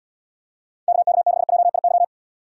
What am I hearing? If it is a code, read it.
Morse code, heard as a keyed tone, about 78 wpm, 700 Hz. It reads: KK6JIY